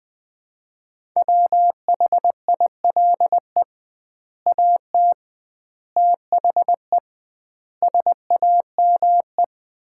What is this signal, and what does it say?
Morse code, heard as a keyed tone, about 20 wpm, 700 Hz. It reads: WHILE AT THE SAME
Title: 2x Word-Spacing - Sets of 5 Words - 20wpm